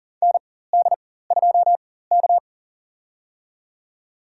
Morse code, heard as a keyed tone, about 40 wpm, spaced as for 19 wpm, 700 Hz. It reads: ND2K
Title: USA Call Signs - 4 Characters - 4x Character Spacing - 40wpm